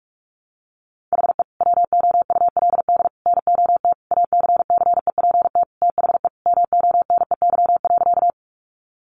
Morse code, heard as a keyed tone, about 44 wpm, 700 Hz. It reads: HE WOULD NOT ACCEPT THE MONEY.
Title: Sentences from Top 1300 Words 45wpm